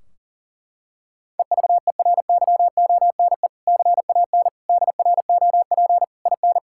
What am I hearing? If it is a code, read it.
Morse code, heard as a keyed tone, about 40 wpm, 700 Hz. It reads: EVERYONE CAN DROP IN
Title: Sentences from Top 1600 Words 40wpm